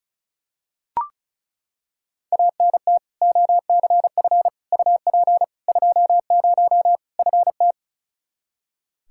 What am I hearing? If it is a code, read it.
Morse code, heard as a keyed tone, about 36 wpm, 700 Hz. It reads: ANT OCF UP 20 FT
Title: Practice QSO - Exchange 2 - Info - 35wpm